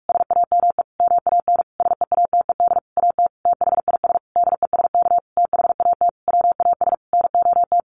Morse code, heard as a keyed tone, about 44 wpm, 700 Hz. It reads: SAME MAN SEATED AT THIS DESK THAT WAS NOT
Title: A Sound Of Thunder 45wpm